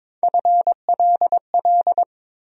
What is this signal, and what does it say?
Morse code, heard as a keyed tone, about 22 wpm, 700 Hz. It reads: FLL
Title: USA Call Sign Postfixes - 3 Letters 22wpm